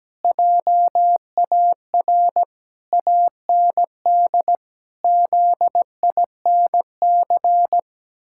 Morse code, heard as a keyed tone, about 17 wpm, 700 Hz. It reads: JAR AND ZINC
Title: Pangram Challenge v2 - 17wpm